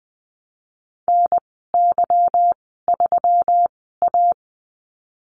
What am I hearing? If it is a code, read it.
Morse code, heard as a keyed tone, about 20 wpm, spaced as for 15 wpm, 700 Hz. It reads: NY3A
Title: 147 Lesson 24g - Letter y - Call Signs 20wpm